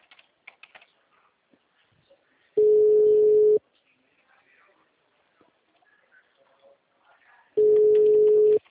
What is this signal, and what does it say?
Ringback tone (Europe, 425 Hz), with speech.